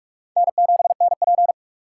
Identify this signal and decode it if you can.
Morse code, heard as a keyed tone, about 45 wpm, 700 Hz. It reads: N7NP
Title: USA Call Signs - 4 Characters - 2x Character Spacing - 45wpm